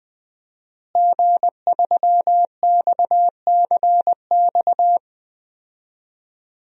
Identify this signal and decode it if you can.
Morse code, heard as a keyed tone, about 21 wpm, 700 Hz. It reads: G3XCX
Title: DXCC-100 Call Signs - 5 Characters - Repeated 3x - 20wpm